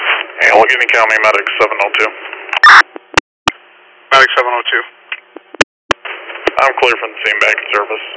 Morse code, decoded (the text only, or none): none